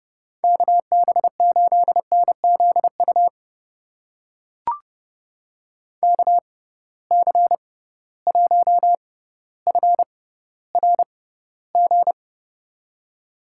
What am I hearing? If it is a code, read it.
Morse code, heard as a keyed tone, about 32 wpm, spaced as for 16 wpm, 700 Hz. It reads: KB8NZU KC1FRG
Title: USA Call Signs - 6 Characters - 6x Character Spacing - 30wpm